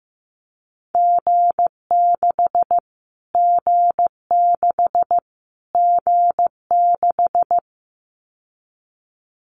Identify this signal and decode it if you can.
Morse code, heard as a keyed tone, about 15 wpm, 700 Hz. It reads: G6 G6 G6
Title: Callsign Prefixes for Top 25 DXCC - Repeated 3x - 15wpm